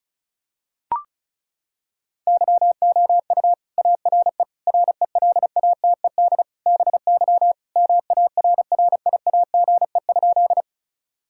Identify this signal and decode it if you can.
Morse code, heard as a keyed tone, about 36 wpm, 700 Hz. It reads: YOU ARE RELATED BY MARRIAGE?
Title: Sentences from Top 1000 Words - Review Part 5 - 35wpm